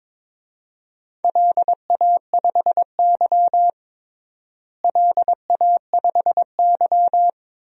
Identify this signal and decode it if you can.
Morse code, heard as a keyed tone, about 22 wpm, 700 Hz. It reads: LA5Y LA5Y
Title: DXCC-100 Call Signs - 4 Characters - Repeated 2x - 22wpm